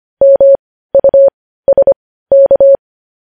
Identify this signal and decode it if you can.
Morse code, heard as a keyed tone, about 25 wpm, spaced as for 15 wpm, 550 Hz. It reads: MUSK
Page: News Headlines in Morse Code at 15 WPM